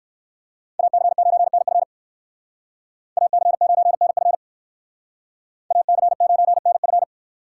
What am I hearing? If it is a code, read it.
Morse code, heard as a keyed tone, about 59 wpm, 700 Hz. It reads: AC9NF AC9NF AC9NF